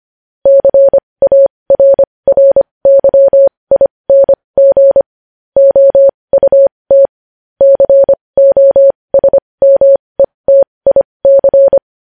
Morse code, heard as a keyed tone, about 25 wpm, spaced as for 20 wpm, 550 Hz. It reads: CARRYING OUT COSMETIC
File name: podcast / News headlines-1x-25x20